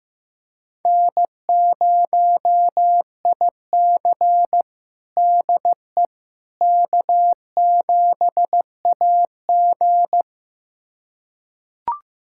Morse code, heard as a keyed tone, about 16 wpm, 700 Hz. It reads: N0IC DE K7AG
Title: USA Callsign DE Callsign - 4 Characters - 15wpm